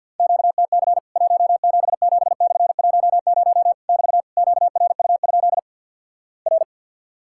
Morse code, heard as a keyed tone, about 50 wpm, 700 Hz. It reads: QTC 177/10 <BT> QRU? R
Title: Worked All Europe DX Contest - QTC Exchange - 50wpm